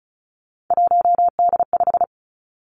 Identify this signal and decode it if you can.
Morse code, heard as a keyed tone, about 36 wpm, 700 Hz. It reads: 1D5
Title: Three Challenging Letter-Number Combinations 35wpm